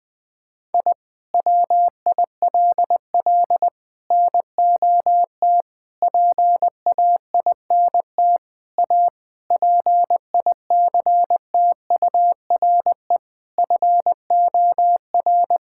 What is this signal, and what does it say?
Morse code, heard as a keyed tone, about 20 wpm, 700 Hz. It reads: I WILL NOT PAINT A PICTURE FOR
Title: Sentences from Top 1000 Words - Review Part 6 - 20wpm